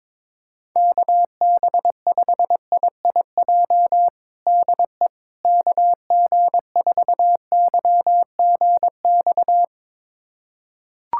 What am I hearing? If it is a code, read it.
Morse code, heard as a keyed tone, about 22 wpm, 700 Hz. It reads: KB5IIJ DE KG4YGX
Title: USA Callsign DE Callsign - 6 Characters - 22wpm